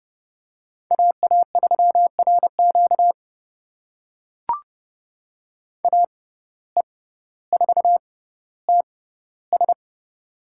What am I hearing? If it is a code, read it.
Morse code, heard as a keyed tone, about 32 wpm, spaced as for 16 wpm, 700 Hz. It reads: AA3RQ AE4TS
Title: USA Call Signs - 5 Characters - 6x Character Spacing - 30wpm